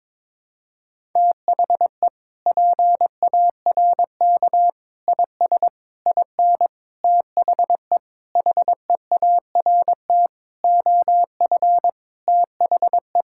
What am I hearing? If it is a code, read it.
Morse code, heard as a keyed tone, about 22 wpm, 700 Hz. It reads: THE PARK IS IN THE HEART OF THE
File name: practice / 22wpm / Sentences with 4-Word Phrases - 22wpm